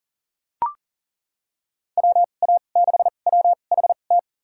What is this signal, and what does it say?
Morse code, heard as a keyed tone, about 40 wpm, spaced as for 29 wpm, 700 Hz. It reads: WA6WHT
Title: USA Call Signs - 6 Characters - 2x Character Spacing - 40wpm